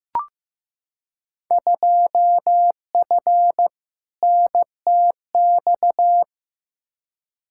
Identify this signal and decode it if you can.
Morse code, heard as a keyed tone, about 15 wpm, 700 Hz. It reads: 2F NTX
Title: Field Day Exchanges - 15wpm